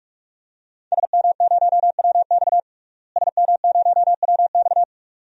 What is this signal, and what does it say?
Morse code, heard as a keyed tone, about 41 wpm, 700 Hz. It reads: SM0WX SM0WX
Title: DXCC-100 Call Signs - 5 Characters - Repeated 2x - 45wpm